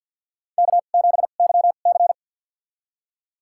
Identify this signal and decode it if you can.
Morse code, heard as a keyed tone, about 53 wpm, spaced as for 37 wpm, 700 Hz. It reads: K7YC
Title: USA Call Signs - 4 Characters - 2x Character Spacing - 50wpm